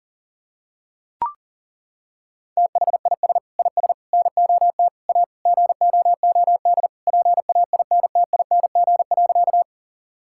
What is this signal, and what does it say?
Morse code, heard as a keyed tone, about 37 wpm, 700 Hz. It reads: THIS IS NOT A GOOD PAINTING.
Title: Sentences from Top 1000 Words - Review Part 6 - 40wpm